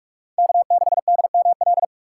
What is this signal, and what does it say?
Morse code, heard as a keyed tone, about 46 wpm, 700 Hz. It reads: K6DML
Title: USA Call Signs - 6x Character Spacing - 45wpm